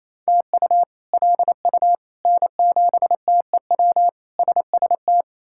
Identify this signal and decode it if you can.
Morse code, heard as a keyed tone, about 27 wpm, 700 Hz. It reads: TU LU N7TEW SST
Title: K1USN SST QSO Practice - 28wpm